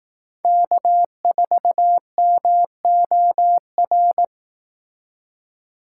Morse code, heard as a keyed tone, about 18 wpm, 700 Hz. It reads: K4MOR
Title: Speed Racing - USA Call Signs 15wpm